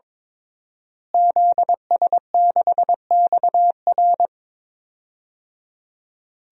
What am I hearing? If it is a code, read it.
Morse code, heard as a keyed tone, about 22 wpm, 700 Hz. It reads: ZS6XR
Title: DXCC-100 Call Signs - 5 Characters - Repeated 3x - 22wpm